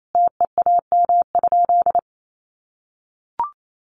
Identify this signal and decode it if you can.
Morse code, heard as a keyed tone, about 28 wpm, 700 Hz. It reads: TEAM?
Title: Sentences from Top 1500 Words 28wpm